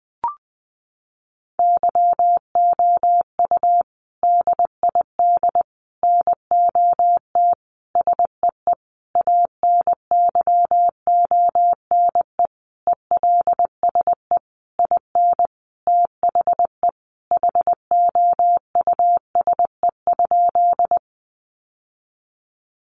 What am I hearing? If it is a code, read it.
Morse code, heard as a keyed tone, about 20 wpm, 700 Hz. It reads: YOU DID NOT SEE ANYONE ELSE IN THE HOUSE?
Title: Sentences from Top 900 Words 20wpm